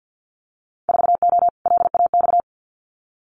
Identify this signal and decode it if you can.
Morse code, heard as a keyed tone, about 50 wpm, 700 Hz. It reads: 4O LAX
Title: Winter Field Day Exchanges Encore 50wpm